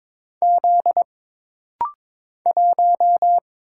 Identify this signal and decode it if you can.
Morse code, heard as a keyed tone, about 22 wpm, 700 Hz. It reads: Z 1
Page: This rapid-fire timing is geared to push your ICR (Instant Character Recognition) skills.